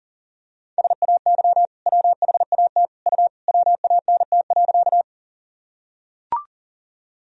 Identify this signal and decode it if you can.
Morse code, heard as a keyed tone, about 40 wpm, 700 Hz. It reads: SAY WHAT U WANT.
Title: Sentences from Top 200 Words and Abbreviations 40wpm